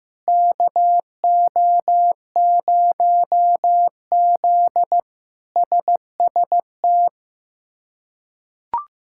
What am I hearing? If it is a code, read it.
Morse code, heard as a keyed tone, about 15 wpm, 700 Hz. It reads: KO0Z SST
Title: K1USN SST QSO Practice - 15wpm